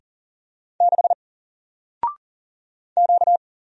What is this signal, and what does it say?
Morse code, heard as a keyed tone, about 40 wpm, 700 Hz. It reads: 6 Q
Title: Single Character (Letter:Number:Punctuation:Prosign) Alt 40wpm